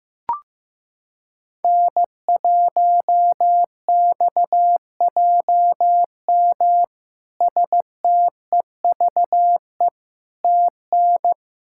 Morse code, heard as a keyed tone, about 16 wpm, 700 Hz. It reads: N1XJM STEVE TN
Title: K1USN SST QSO Practice - 15wpm